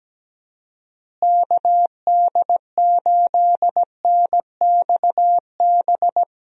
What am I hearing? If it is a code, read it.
Morse code, heard as a keyed tone, about 17 wpm, 700 Hz. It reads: KD8NXB
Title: USA Call Signs - 6 Characters - Repeated 3x - 17wpm